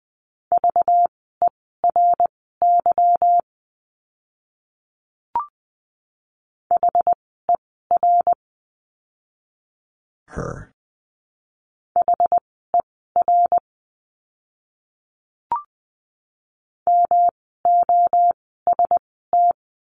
Morse code, heard as a keyed tone, about 20 wpm, spaced as for 15 wpm, 700 Hz. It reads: VERY HER HER MOST
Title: 230 Lesson 37f - Letter x - Words 20wpm